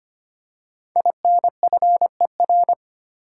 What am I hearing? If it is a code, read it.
Morse code, heard as a keyed tone, about 25 wpm, 700 Hz. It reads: INFER
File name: 5 Letter Words - First 3 Letters Given - 25wpm